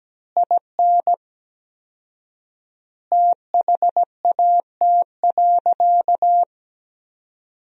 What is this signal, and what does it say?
Morse code, heard as a keyed tone, about 17 wpm, 700 Hz. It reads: IN THAT.
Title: 4x Word-Spacing - Sentences from Top 200 Words - 17wpm